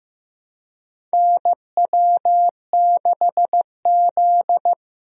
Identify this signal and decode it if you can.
Morse code, heard as a keyed tone, about 15 wpm, 700 Hz. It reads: NW6Z
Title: USA Call Signs - 4 Characters - Repeated 3x - 15wpm